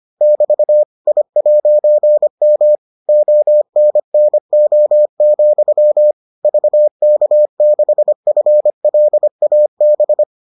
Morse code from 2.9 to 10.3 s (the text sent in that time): ONNO, VK6FLAB